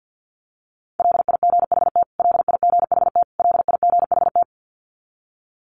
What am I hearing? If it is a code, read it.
Morse code, heard as a keyed tone, about 49 wpm, 700 Hz. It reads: LIGHT LIGHT LIGHT
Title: Top 500 Words Repeated 50wpm